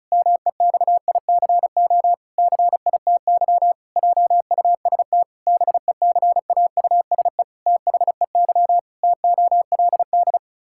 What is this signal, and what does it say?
Morse code, heard as a keyed tone, about 36 wpm, 700 Hz. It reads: MEXICO CITY JUST BECAUSE THEY TOLD
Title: Pangram Challenge v2 - 35wpm